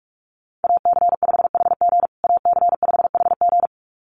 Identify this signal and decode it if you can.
Morse code, heard as a keyed tone, about 44 wpm, 700 Hz. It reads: AC5HG AC5HG
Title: USA Call Signs Repeated 2x 45wpm